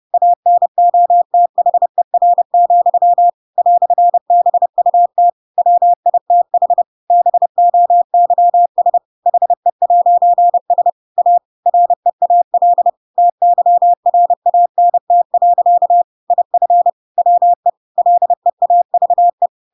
Morse code, heard as a keyed tone, about 30 wpm, 700 Hz. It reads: ANOTHER, "BUT WITH BOYS HE'S A REAL TYRANT. IF WE LEAVE
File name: Pinocchio / Pinocchio ch3 30 30 ik1hge